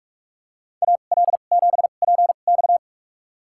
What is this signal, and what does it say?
Morse code, heard as a keyed tone, about 45 wpm, spaced as for 32 wpm, 700 Hz. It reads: AL7PX
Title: USA Call Signs - 2x Character Spacing - 45wpm